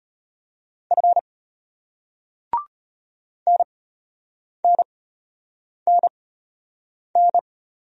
Morse code, heard as a keyed tone, about 34 wpm, spaced as for 8 wpm, 700 Hz. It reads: F NNNN